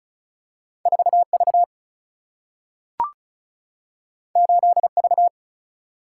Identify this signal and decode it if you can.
Morse code, heard as a keyed tone, about 36 wpm, 700 Hz. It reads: 4V 8V
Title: Single-Letter Single-Number 35wpm